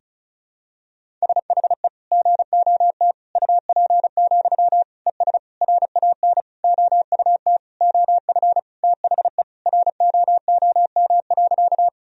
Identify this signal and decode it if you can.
Morse code, heard as a keyed tone, about 36 wpm, 700 Hz. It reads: SHE GOT UP, ES RAN OUT OF THE ROOM.
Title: Sentences from Top 500 Words and Abbreviations 35wpm